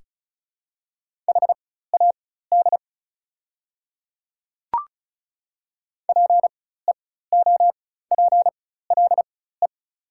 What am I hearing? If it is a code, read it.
Morse code, heard as a keyed tone, about 36 wpm, spaced as for 16 wpm, 700 Hz. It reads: HAD PEOPLE
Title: 4x Character Spacing - Top 100 Words - 35wpm